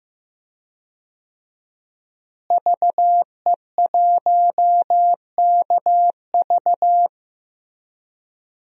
Morse code, heard as a keyed tone, about 15 wpm, 700 Hz. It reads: VE1KV
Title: DXCC-100 Call Signs - 5 Characters - Repeated 2x - 15wpm